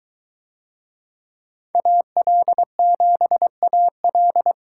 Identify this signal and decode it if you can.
Morse code, heard as a keyed tone, about 21 wpm, 700 Hz. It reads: AL7AL